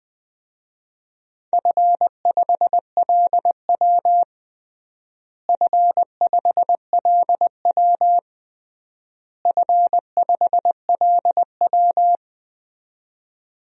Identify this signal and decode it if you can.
Morse code, heard as a keyed tone, about 20 wpm, 700 Hz. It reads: F5LW F5LW F5LW
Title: DXCC-100 Call Signs - 4 Characters - Repeated 3x - 20wpm